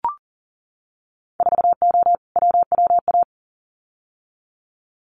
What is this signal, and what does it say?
Morse code, heard as a keyed tone, about 41 wpm, 700 Hz. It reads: 4O WWA
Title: Winter Field Day Exchanges 40wpm